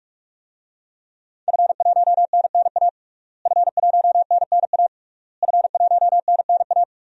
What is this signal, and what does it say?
Morse code, heard as a keyed tone, about 46 wpm, 700 Hz. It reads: F1NNA F1NNA F1NNA
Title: DXCC-100 Call Signs - 5 Characters - Repeated 3x - 45wpm